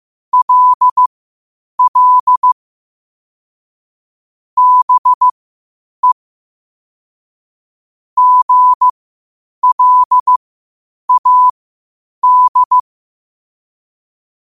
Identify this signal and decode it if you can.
Morse code, heard as a keyed tone, about 15 wpm, spaced as for 8 wpm, 1000 Hz. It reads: LL BE GLAD
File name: qotdfiles / 8wpm